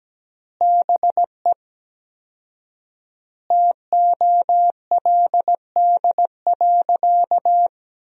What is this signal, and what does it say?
Morse code, heard as a keyed tone, about 17 wpm, 700 Hz. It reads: BE TOLD.
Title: 4x Word-Spacing - Sentences from Top 400 Words - 17wpm